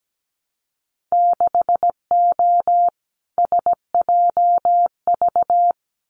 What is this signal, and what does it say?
Morse code, heard as a keyed tone, about 17 wpm, 700 Hz. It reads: 6O SJV
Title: Winter Field Day Exchanges Encore 17wpm